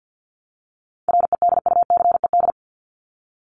Morse code, heard as a keyed tone, about 50 wpm, 700 Hz. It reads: REDUCED
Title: ED Suffix - Top 100 Words with Base Words 50wpm